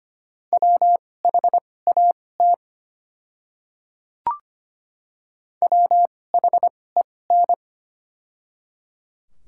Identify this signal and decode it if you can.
Morse code, heard as a keyed tone, about 25 wpm, spaced as for 18 wpm, 690 Hz. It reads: WHAT WHEN
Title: 2x Character Spacing - Top 100 Words - 25wpm